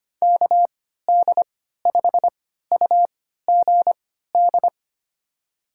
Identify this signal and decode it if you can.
Morse code, heard as a keyed tone, about 25 wpm, spaced as for 14 wpm, 700 Hz. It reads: KD5UGD
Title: USA Call Signs - 6 Characters - 3x Character Spacing - 25wpm